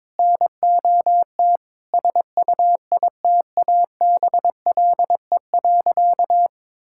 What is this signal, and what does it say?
Morse code, heard as a keyed tone, about 22 wpm, 700 Hz. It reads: NOT SUITABLE.
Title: Sentences from Top 1600 Words 22wpm